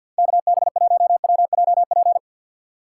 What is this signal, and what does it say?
Morse code, heard as a keyed tone, about 51 wpm, 700 Hz. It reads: KB1WJP
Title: USA Call Signs - 6 Characters - 6x Character Spacing - 50wpm